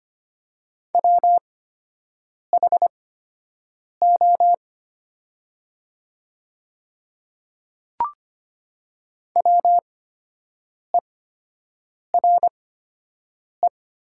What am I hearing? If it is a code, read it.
Morse code, heard as a keyed tone, about 25 wpm, spaced as for 7 wpm, 700 Hz. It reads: WHO WERE
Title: 8x Character Spacing - Top 100 Words - 25wpm